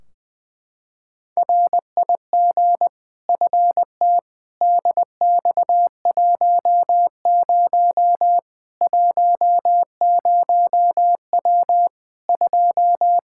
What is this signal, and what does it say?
Morse code, heard as a keyed tone, about 20 wpm, 700 Hz. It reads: RIG FT DX10 10W 2
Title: Practice QSO - Exchange 2 - Info - 20wpm